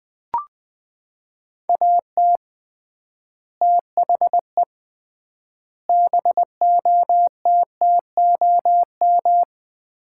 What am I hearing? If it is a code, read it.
Morse code, heard as a keyed tone, about 21 wpm, 700 Hz. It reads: AT THE BOTTOM
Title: 3x Word-Spacing - Sets of 5 Words - 20wpm